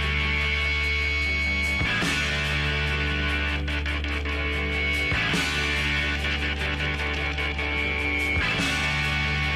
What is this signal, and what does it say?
Dial-up modem handshake.
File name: wrct-log hi- 16